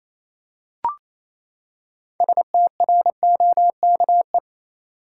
Morse code, heard as a keyed tone, about 28 wpm, 700 Hz. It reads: STROKE
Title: Six Letter Words - 28wpm